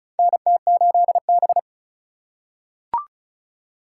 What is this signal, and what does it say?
Morse code, heard as a keyed tone, about 36 wpm, 700 Hz. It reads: NT8B
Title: USA Call Signs - 4 Characters - 2x Character Spacing - 35wpm